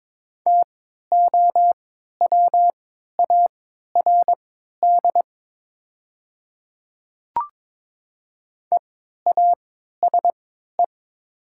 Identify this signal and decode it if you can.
Morse code, heard as a keyed tone, about 22 wpm, spaced as for 13 wpm, 700 Hz. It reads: TOWARD EASE
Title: 3x Character Spacing - Top 500 Words - 22wpm